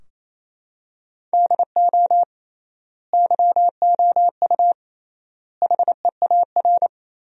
Morse code, heard as a keyed tone, about 28 wpm, 700 Hz. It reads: DO YOU HEAR
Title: 3x Word-Spacing - Sentences from Top 300 Words - 28wpm